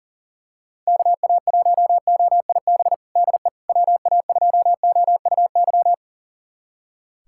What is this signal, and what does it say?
Morse code, heard as a keyed tone, about 37 wpm, 700 Hz. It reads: KA1OIB DE WA2OUY
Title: USA Callsign DE Callsign 40wpm